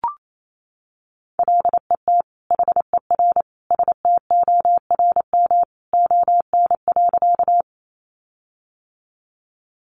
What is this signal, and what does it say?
Morse code, heard as a keyed tone, about 28 wpm, 700 Hz. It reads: LET HER STORM ON.
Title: Sentences from Top 1500 Words 28wpm